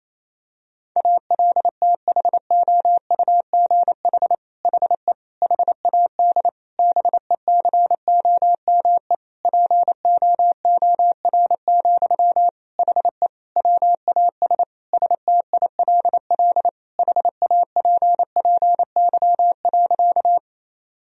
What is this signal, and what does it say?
Morse code, heard as a keyed tone, about 27 wpm, 700 Hz. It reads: ALTHOUGH HE HAD BECOME POOR, HE WAS STILL HAPPY.
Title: Sentences from Top 1000 Words - Review Part 3 - 28wpm